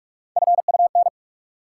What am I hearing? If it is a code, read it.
Morse code, heard as a keyed tone, about 46 wpm, 700 Hz. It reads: FUN
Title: Sentences from Top 700 Words - New Word Drill - 45wpm